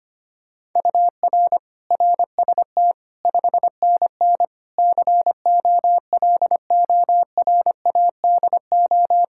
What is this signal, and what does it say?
Morse code, heard as a keyed tone, about 25 wpm, 700 Hz. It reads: UR RST 5NN COLORADO
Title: Practice QSO - Exchange 1 - Beginning - 25wpm